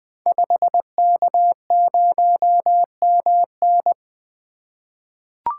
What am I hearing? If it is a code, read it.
Morse code, heard as a keyed tone, about 20 wpm, 700 Hz. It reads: 5K0MN
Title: Code Groups - 5 Letter:Numbers - 20wpm